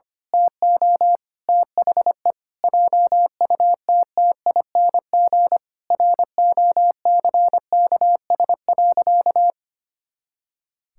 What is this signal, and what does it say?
Morse code, heard as a keyed tone, about 25 wpm, 700 Hz. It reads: TO THE JUTTING ROCKS.
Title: Pangram Challenge v2 - 25wpm